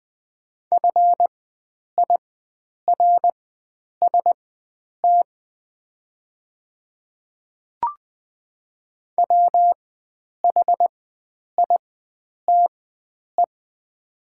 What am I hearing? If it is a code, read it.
Morse code, heard as a keyed tone, about 20 wpm, spaced as for 9 wpm, 700 Hz. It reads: FIRST WHITE
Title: 4x Character Spacing - Top 300 Words - 20wpm